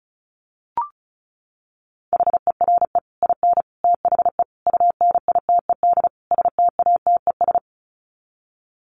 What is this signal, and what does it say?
Morse code, heard as a keyed tone, about 34 wpm, 700 Hz. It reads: HERE IN THE UNITED STATES
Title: Sets of 5 Words 35wpm